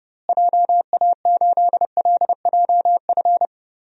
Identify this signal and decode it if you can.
Morse code, heard as a keyed tone, about 29 wpm, 700 Hz. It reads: JA8LJF
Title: DXCC-100 Call Signs - 6 Characters - Repeated 3x - 30wpm